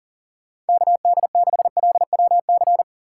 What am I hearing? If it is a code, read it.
Morse code, heard as a keyed tone, about 49 wpm, 700 Hz. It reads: KD6LWC
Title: Speed Racing - USA 2x3 Call Signs 40wpm